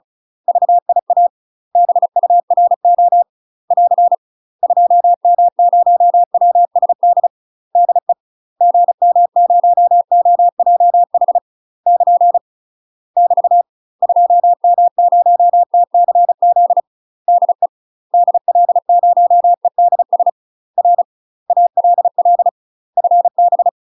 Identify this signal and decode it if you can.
Morse code, heard as a keyed tone, about 34 wpm, 700 Hz. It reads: VIA BURO <AR> 2M0WSD DE GM0OJH <KN> <BT> 2M0TCZ DE DL0EDS R ALL FB
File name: UK MASTER QSOs TRAINING / 35WPM / UK MASTER QSOs 0005